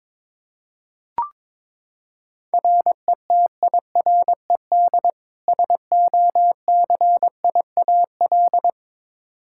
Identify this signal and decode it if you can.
Morse code, heard as a keyed tone, about 22 wpm, 700 Hz. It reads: RETIRED SOCIAL
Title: K1USN SST QSO Practice 22wpm